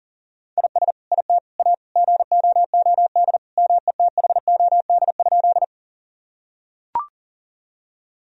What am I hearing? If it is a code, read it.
Morse code, heard as a keyed tone, about 40 wpm, 700 Hz. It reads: IS IT A GOOD METHOD?
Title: Sentences from Top 1000 Words - Review Part 3 - 40wpm